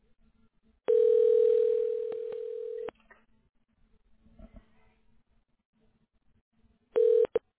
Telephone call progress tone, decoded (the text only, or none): ringback tone